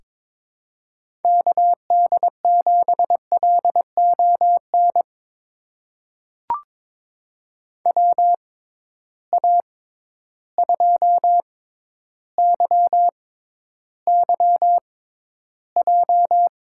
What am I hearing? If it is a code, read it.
Morse code, heard as a keyed tone, about 22 wpm, spaced as for 11 wpm, 700 Hz. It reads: KD7LON WA2YYJ
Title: USA Call Signs - 6 Characters - 6x Character Spacing - 22wpm